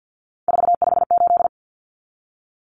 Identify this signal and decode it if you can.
Morse code, heard as a keyed tone, about 50 wpm, 700 Hz. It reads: 458
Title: Numbers - 3-Digit 50wpm